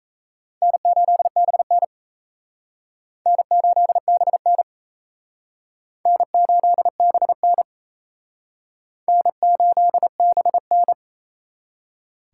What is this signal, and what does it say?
Morse code, heard as a keyed tone, about 36 wpm, 700 Hz. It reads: N8BN N8BN N8BN N8BN